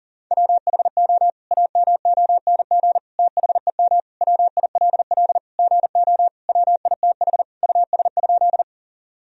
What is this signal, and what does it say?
Morse code, heard as a keyed tone, about 37 wpm, 700 Hz. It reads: WHO AMONG THEM WILL GO WITH US?
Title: Sentences from Top 1000 Words - Review Part 3 - 40wpm